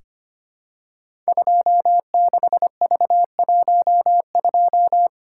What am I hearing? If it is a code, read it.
Morse code, heard as a keyed tone, about 25 wpm, 700 Hz. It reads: 26V12
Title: Code Groups - 5 Letter:Numbers - 25wpm